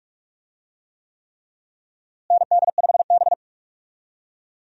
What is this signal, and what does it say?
Morse code, heard as a keyed tone, about 46 wpm, 700 Hz. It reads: ND5B
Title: USA Call Signs - 4 Characters - Repeated 3x - 45wpm